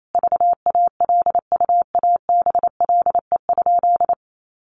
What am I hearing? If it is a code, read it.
Morse code, heard as a keyed tone, about 28 wpm, 700 Hz. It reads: VALUABLE?
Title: Sentences from Top 1100 Words 28wpm